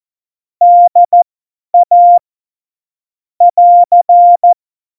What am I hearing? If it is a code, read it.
Morse code, heard as a keyed tone, about 14 wpm, spaced as for 10 wpm, 700 Hz. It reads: DA <AR>
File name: Morsekurs / Witze / Witz 15. August 17, Tempo 50 F70